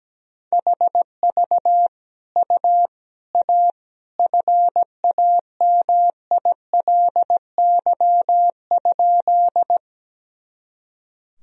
Morse code, heard as a keyed tone, about 17 wpm, 700 Hz. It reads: HV U A FAMILY?
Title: Sentences from Top 400 Words and Abbreviations 17wpm